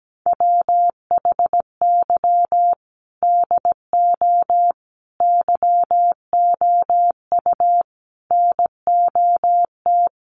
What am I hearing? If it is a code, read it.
Morse code, heard as a keyed tone, about 17 wpm, 700 Hz. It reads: WHY DO YOU NOT
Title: Sentences from Top 1400 Words 17wpm